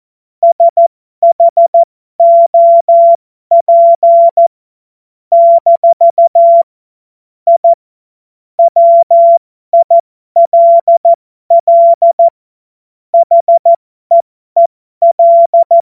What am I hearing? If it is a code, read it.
Morse code, heard as a keyed tone, about 14 wpm, spaced as for 12 wpm, 680 Hz. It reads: SHOP - I WILL HEEL
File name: Morsekurs / Witze